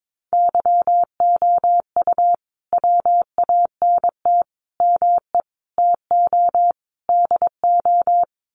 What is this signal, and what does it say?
Morse code, heard as a keyed tone, about 22 wpm, 700 Hz. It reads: YOU WANT ME TO DO
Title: Sets of 5 Words Encore2 22wpm